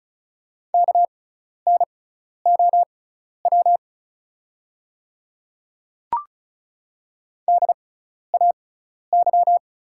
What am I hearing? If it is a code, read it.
Morse code, heard as a keyed tone, about 35 wpm, spaced as for 12 wpm, 690 Hz. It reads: KNOW DAY
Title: 6x Character Spacing - Top 100 Words - 35wpm